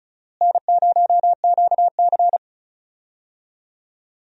Morse code, heard as a keyed tone, about 34 wpm, 700 Hz. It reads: N0QC
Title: USA Call Signs - 4 Characters - Repeated 3x - 35wpm